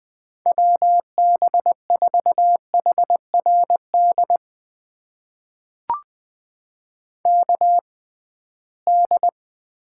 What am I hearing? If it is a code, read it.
Morse code, heard as a keyed tone, about 20 wpm, spaced as for 15 wpm, 700 Hz. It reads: WB4HRD KD
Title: USA Call Signs - 6 Characters - 6x Character Spacing - 20wpm